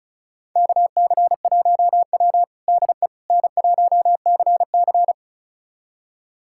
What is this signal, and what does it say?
Morse code, heard as a keyed tone, about 36 wpm, 700 Hz. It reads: KC1W DE N1CC
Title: USA Callsign DE Callsign - 4 Characters - 35wpm